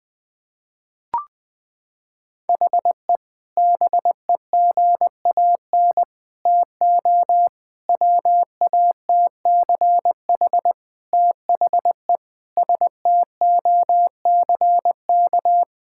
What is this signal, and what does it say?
Morse code, heard as a keyed tone, about 20 wpm, 700 Hz. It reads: HE BEGAN TO WATCH THE STOCK
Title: Sentences from Top 1000 Words - Review Part 6 - 20wpm